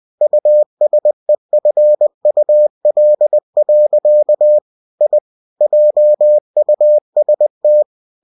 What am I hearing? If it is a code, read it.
Morse code, heard as a keyed tone, about 20 wpm, 600 Hz. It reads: USEFUL. I JUST
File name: cw-books / winnie-the-pooh / winnie-20wpm-page-0087